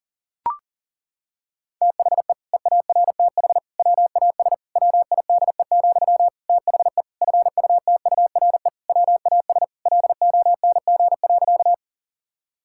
Morse code, heard as a keyed tone, about 40 wpm, 700 Hz. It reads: THE EARTH WAS WIDE, THE FUTURE WAS LONG.
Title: Sentences from Top 1000 Words - Review Part 6 - 40wpm